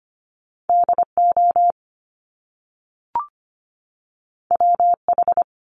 Morse code, heard as a keyed tone, about 25 wpm, 700 Hz. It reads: DO WH
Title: Top Two Letter Combinations in Words 25wpm